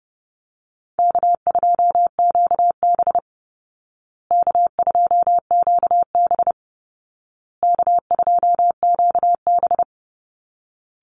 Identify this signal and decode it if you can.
Morse code, heard as a keyed tone, about 30 wpm, 700 Hz. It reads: K2QB K2QB K2QB
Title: Easy Call Signs Repeated 3x - 30wpm